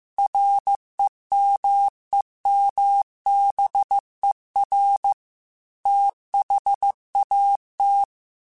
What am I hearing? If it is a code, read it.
Morse code, heard as a keyed tone, about 15 wpm, 800 Hz. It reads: REMEMBER THAT
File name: Technical Subjects / Technical Subjects-mp3s / Sw Pwr Sup @13